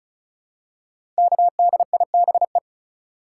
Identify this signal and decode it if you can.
Morse code, heard as a keyed tone, about 36 wpm, 700 Hz. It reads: KDIBE
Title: Code Groups - 5 Letters - 35wpm